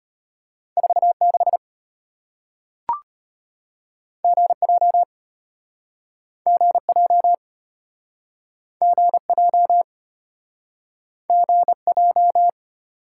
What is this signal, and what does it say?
Morse code, heard as a keyed tone, about 35 wpm, 700 Hz. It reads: 46 GJ GJ GJ GJ